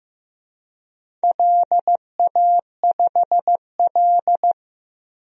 Morse code, heard as a keyed tone, about 15 wpm, 700 Hz. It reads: LA5L
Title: DXCC-100 Call Signs - 4 Characters - Repeated 2x - 15wpm